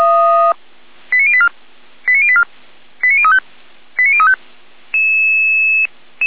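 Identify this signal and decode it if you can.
Five-tone selective call (ZVEI), heard as quick runs of five tones, tones of about 70 ms. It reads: 89084 89084 89035 89035
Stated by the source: ZVEI five-tone selective call received on SDR, feuerwehr-orlamuende.org fire brigade audio server.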